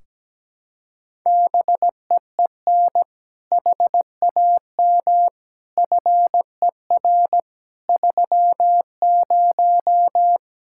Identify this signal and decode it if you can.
Morse code, heard as a keyed tone, about 17 wpm, 700 Hz. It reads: BEEN HAM FER 30
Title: Practice QSO - Exchange 3 - More Info - 17wpm